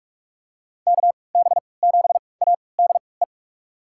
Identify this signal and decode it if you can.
Morse code, heard as a keyed tone, about 46 wpm, spaced as for 25 wpm, 700 Hz. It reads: KB7ADE
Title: USA Call Signs - 6 Characters - 3x Character Spacing - 45wpm